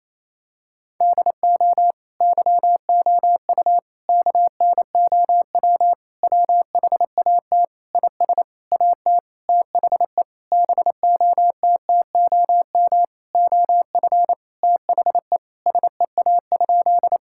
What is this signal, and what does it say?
Morse code, heard as a keyed tone, about 28 wpm, 700 Hz. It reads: DO YOU KNOW WHAT IS AT THE BOTTOM OF THE SEA?
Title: Sentences from Top 1000 Words - Review Part 6 - 28wpm